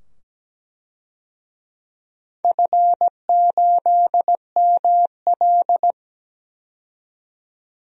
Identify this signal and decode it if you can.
Morse code, heard as a keyed tone, about 17 wpm, 700 Hz. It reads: F8ML
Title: DXCC-100 Call Signs - 4 Characters - 17wpm